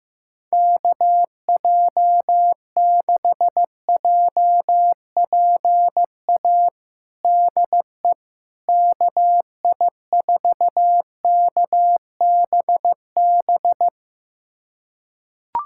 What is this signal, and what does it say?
Morse code, heard as a keyed tone, about 15 wpm, 700 Hz. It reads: KJ6JPA DE KI4KBB
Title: USA Callsign DE Callsign - 6 Characters - 15wpm